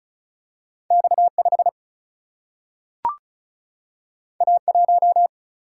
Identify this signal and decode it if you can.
Morse code, heard as a keyed tone, about 34 wpm, 700 Hz. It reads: X5 A1
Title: Single-Letter Single-Number 35wpm